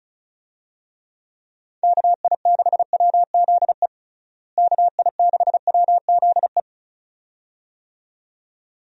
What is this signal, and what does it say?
Morse code, heard as a keyed tone, about 36 wpm, 700 Hz. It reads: KI6WZE KI6WZE
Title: USA Call Signs - 6 Characters - Repeated 2x - 35wpm